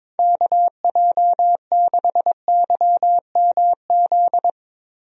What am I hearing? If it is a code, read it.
Morse code, heard as a keyed tone, about 22 wpm, 700 Hz. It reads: KJ6YMZ
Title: USA Call Signs - 6 Characters - 22wpm